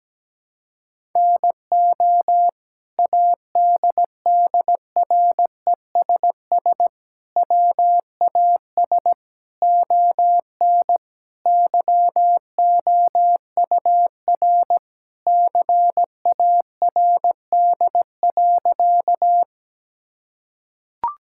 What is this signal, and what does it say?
Morse code, heard as a keyed tone, about 17 wpm, 700 Hz. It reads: NO ADDRESS WAS ON YOUR CARD.
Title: Sentences from Top 1000 Words - Review Part 6 - 17wpm